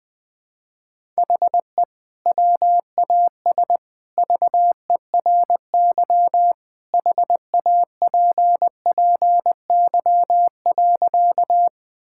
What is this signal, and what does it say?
Morse code, heard as a keyed tone, about 20 wpm, 700 Hz. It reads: HE WAS VERY HAPPY.
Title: Sentences with 2-Word Phrases - Spoken Beginning - 20wpm